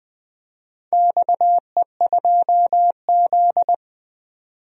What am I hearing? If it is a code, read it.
Morse code, heard as a keyed tone, about 20 wpm, 700 Hz. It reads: XE2Z
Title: DXCC-100 Call Signs - 4 Characters - 20wpm